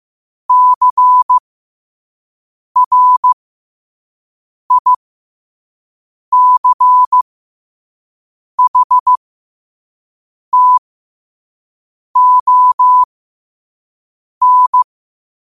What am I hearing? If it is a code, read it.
Morse code, heard as a keyed tone, about 15 wpm, spaced as for 5 wpm, 1000 Hz. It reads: CRICHTON
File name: qotdfiles / 5wpm